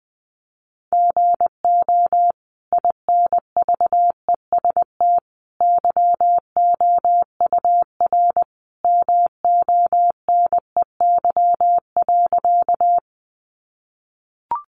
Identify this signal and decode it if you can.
Morse code, heard as a keyed tone, about 20 wpm, 700 Hz. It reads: GO INVEST YOUR MONEY.
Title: Sentences from Top 1100 Words 20wpm